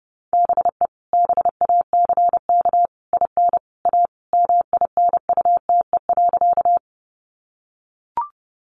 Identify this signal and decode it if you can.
Morse code, heard as a keyed tone, about 30 wpm, 700 Hz. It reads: BE BACK IN A MINUTE.
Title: Sentences from Top 500 Words 30wpm